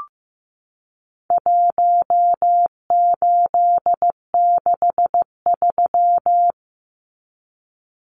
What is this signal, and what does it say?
Morse code, heard as a keyed tone, about 15 wpm, 700 Hz. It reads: 1863
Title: Numbers - Contest Style - 4-Digit 15wpm